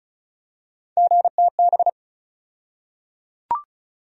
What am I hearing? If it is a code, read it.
Morse code, heard as a keyed tone, about 34 wpm, 700 Hz. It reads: GTB
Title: USA Call Sign Postfixes - 3 Letters 35wpm